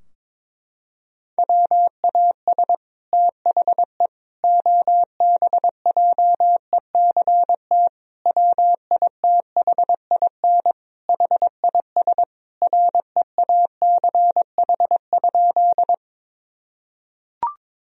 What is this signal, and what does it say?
Morse code, heard as a keyed tone, about 22 wpm, 700 Hz. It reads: WAS THE OBJECT WITHIN HIS REACH?
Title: Sentences from Top 1000 Words - Review Part 4 - 22wpm